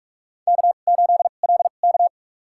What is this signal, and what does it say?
Morse code, heard as a keyed tone, about 43 wpm, spaced as for 32 wpm, 700 Hz. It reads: K8LK
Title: USA Call Signs - 4 Characters - 2x Character Spacing - 45wpm